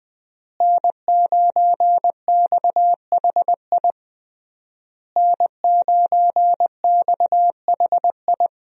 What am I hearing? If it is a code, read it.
Morse code, heard as a keyed tone, about 20 wpm, 700 Hz. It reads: N9XHI N9XHI
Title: USA Call Signs - 5 Characters - Repeated 3x - 20wpm